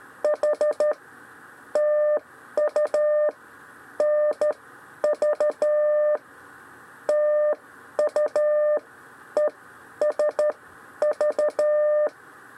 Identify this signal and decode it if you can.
Morse code, heard as a keyed tone, about 13 wpm, spaced as for 9 wpm, 600 Hz. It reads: HTUNV TUESV
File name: 6 lezione cw 1